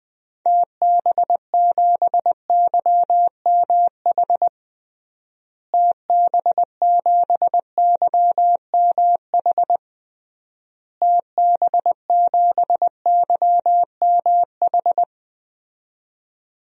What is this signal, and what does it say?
Morse code, heard as a keyed tone, about 21 wpm, 700 Hz. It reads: TB7YMH TB7YMH TB7YMH
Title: DXCC-100 Call Signs - 6 Characters - Repeated 3x - 20wpm